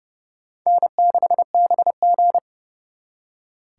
Morse code, heard as a keyed tone, about 37 wpm, 700 Hz. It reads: N6BG
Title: Speed Racing - USA 1x2 Call Signs 30wpm